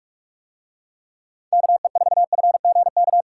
Morse code, heard as a keyed tone, about 41 wpm, 700 Hz. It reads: KE4FGK